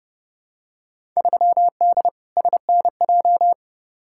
Morse code, heard as a keyed tone, about 30 wpm, 700 Hz. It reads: 3D SNJ
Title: Field Day Exchanges Encore - 30wpm